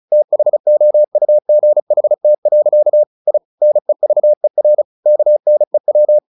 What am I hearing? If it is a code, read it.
Morse code, heard as a keyed tone, about 34 wpm, 600 Hz. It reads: THOUGHT. I NEVER KNEW
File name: the old man and the sea 2 35wpm